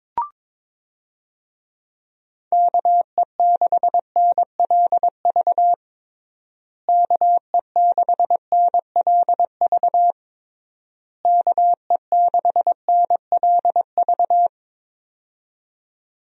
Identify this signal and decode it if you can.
Morse code, heard as a keyed tone, about 22 wpm, 700 Hz. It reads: KE6NLV KE6NLV KE6NLV
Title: USA Call Signs - 6 Characters - Repeated 3x - 22wpm